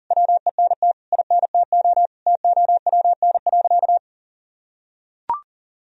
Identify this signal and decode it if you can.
Morse code, heard as a keyed tone, about 40 wpm, 700 Hz. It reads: WENT INTO TOWN.
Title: Sentences from Top 1000 Words - Review Part 2 - 40wpm